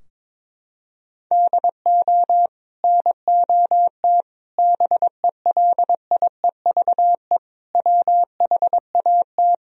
Morse code, heard as a keyed tone, about 22 wpm, 700 Hz. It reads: DO NOT BELIEVE WHAT
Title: Sentences from Top 1000 Words - Review Part 4 - 22wpm